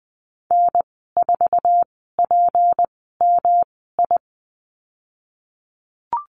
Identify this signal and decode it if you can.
Morse code, heard as a keyed tone, about 20 wpm, spaced as for 15 wpm, 700 Hz. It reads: N4PMI
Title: 154 Lesson 25g - Letter p - Call Signs 20wpm